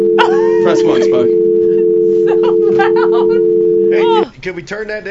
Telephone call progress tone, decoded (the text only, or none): dial tone